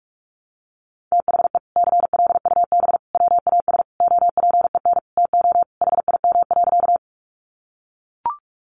Morse code, heard as a keyed tone, about 44 wpm, 700 Hz. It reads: THE CLUB WAS OPEN TO HIM.
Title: Sentences from Top 1200 Words 45wpm